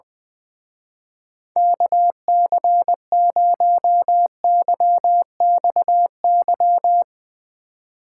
Morse code, heard as a keyed tone, about 21 wpm, 700 Hz. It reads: KC0YXY
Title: Speed Racing - USA Call Signs 20wpm